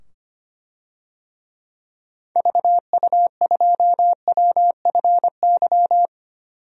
Morse code, heard as a keyed tone, about 25 wpm, 700 Hz. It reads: VU2WFY
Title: DXCC-100 Call Signs - 6 Characters - 25wpm